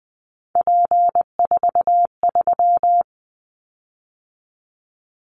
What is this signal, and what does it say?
Morse code, heard as a keyed tone, about 20 wpm, 700 Hz. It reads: P43
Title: Callsign Prefixes for Top 100 DXCC - 20wpm